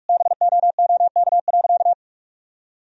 Morse code, heard as a keyed tone, about 45 wpm, 700 Hz. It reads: BOOK.
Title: Sentences from Top 1000 Words - Review Part 1 - 45wpm